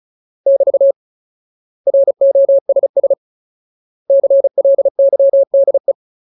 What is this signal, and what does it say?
Morse code, heard as a keyed tone, about 34 wpm, 550 Hz. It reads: <BT> ROSS CLYDE
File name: Ultimate Word List / 35WPM Ultimate